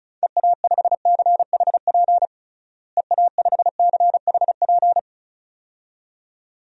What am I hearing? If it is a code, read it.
Morse code, heard as a keyed tone, about 36 wpm, 700 Hz. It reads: EA5CHP EA5CHP
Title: DXCC-100 Call Signs - 6 Characters - Repeated 2x - 35wpm